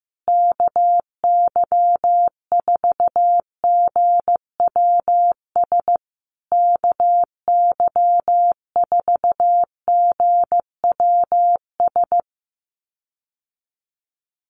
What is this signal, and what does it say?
Morse code, heard as a keyed tone, about 15 wpm, 700 Hz. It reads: KY4GWS KY4GWS
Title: USA Call Signs Repeated 3x 15wpm